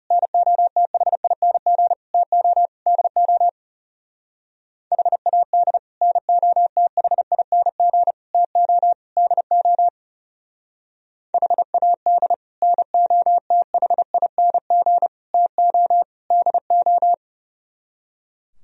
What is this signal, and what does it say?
Morse code, heard as a keyed tone, about 38 wpm, 700 Hz. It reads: NOTHING TO DO HAD NOTHING TO DO HAD NOTHING TO DO